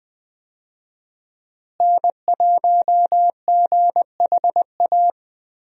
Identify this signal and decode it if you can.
Morse code, heard as a keyed tone, about 20 wpm, 700 Hz. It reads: N1GHA
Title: USA Call Signs - 5 Characters - Repeated 2x - 20wpm